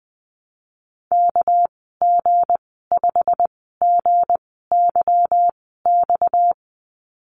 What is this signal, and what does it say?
Morse code, heard as a keyed tone, about 20 wpm, spaced as for 15 wpm, 700 Hz. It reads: KG5GYX
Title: 238 Lesson 38g - Letter q - Call Signs 20wpm